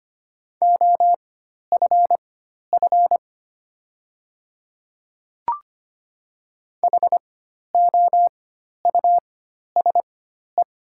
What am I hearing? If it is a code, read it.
Morse code, heard as a keyed tone, about 25 wpm, spaced as for 12 wpm, 690 Hz. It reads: OFF HOUSE
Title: 4x Character Spacing - Top 200 Words - 25wpm